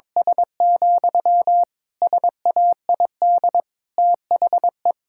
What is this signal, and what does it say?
Morse code, heard as a keyed tone, about 22 wpm, 700 Hz. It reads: S, SAID THE
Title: Sentences from Top 1600 Words 22wpm